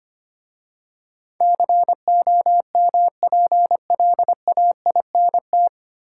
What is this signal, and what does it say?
Morse code, heard as a keyed tone, about 25 wpm, 700 Hz. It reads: COMPLAINT
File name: practice / 25wpm / 9 Letter Words - First 3 Letters Given - 25wpm